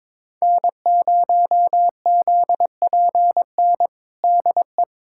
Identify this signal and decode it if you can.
Morse code, heard as a keyed tone, about 22 wpm, 700 Hz. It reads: N0ZPN DE
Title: USA Callsign DE Callsign 22wpm